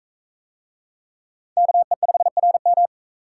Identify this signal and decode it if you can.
Morse code, heard as a keyed tone, about 43 wpm, 700 Hz. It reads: KE5RM